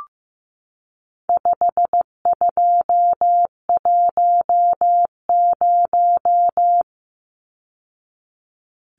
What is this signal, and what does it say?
Morse code, heard as a keyed tone, about 15 wpm, 700 Hz. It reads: 5210
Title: Numbers - 4-Digit 15wpm